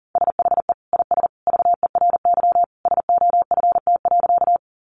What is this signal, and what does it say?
Morse code, heard as a keyed tone, about 40 wpm, 700 Hz. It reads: SHE IS VERY SOFT.
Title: Easy Phrases 40wpm